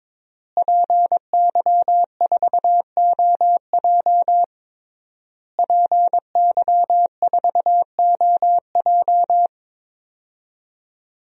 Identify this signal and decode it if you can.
Morse code, heard as a keyed tone, about 22 wpm, 700 Hz. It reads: PY4OJ PY4OJ
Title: DXCC-100 Call Signs - 5 Characters - Repeated 3x - 22wpm